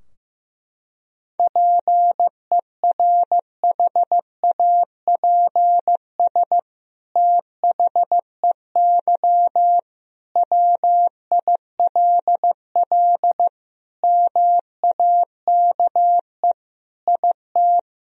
Morse code, heard as a keyed tone, about 15 wpm, 700 Hz. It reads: PERHAPS THEY WILL MAKE IT
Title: Sentences from Top 1000 Words - Review Part 2 - 15wpm